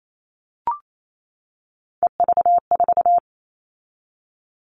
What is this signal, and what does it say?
Morse code, heard as a keyed tone, about 28 wpm, 700 Hz. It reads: EV4
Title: All Cut Numbers - 3-Digit 28wpm